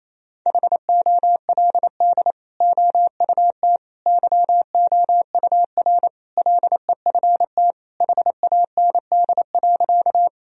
Morse code, heard as a keyed tone, about 28 wpm, 700 Hz. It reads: HOLD OUT YOUR LEFT HAND.
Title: Sentences from Top 1000 Words - Review Part 2 - 28wpm